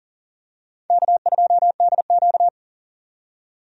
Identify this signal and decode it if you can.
Morse code, heard as a keyed tone, about 37 wpm, 700 Hz. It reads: K2DQ
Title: USA Call Signs - 4 Characters - 40wpm